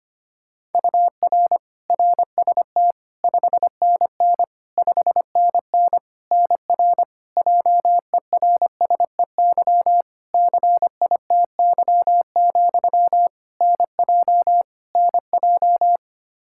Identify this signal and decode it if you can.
Morse code, heard as a keyed tone, about 25 wpm, 700 Hz. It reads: UR RST 5NN 5NN NR JERSEY CITY, NJ NJ
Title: Practice QSO - Exchange 1 - Beginning - 25wpm